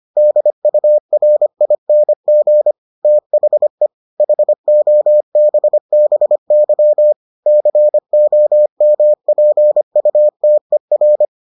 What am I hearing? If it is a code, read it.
Morse code, heard as a keyed tone, about 25 wpm, 600 Hz. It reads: DURING THE HOBBY COMPUTER